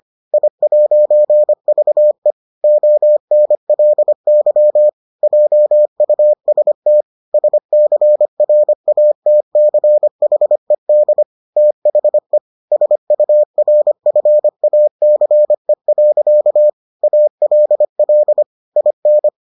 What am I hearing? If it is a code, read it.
Morse code, heard as a keyed tone, about 25 wpm, 600 Hz. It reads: I'VE ONLY JUST SCRATCHED THE SURFACE. ALL IN